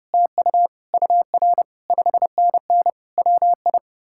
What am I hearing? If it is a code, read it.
Morse code, heard as a keyed tone, about 40 wpm, 700 Hz. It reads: TU UR 5NN WI